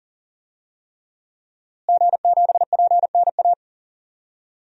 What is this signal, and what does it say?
Morse code, heard as a keyed tone, about 37 wpm, 700 Hz. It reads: G7PNA
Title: DXCC-100 Call Signs - 5 Characters - Repeated 3x - 40wpm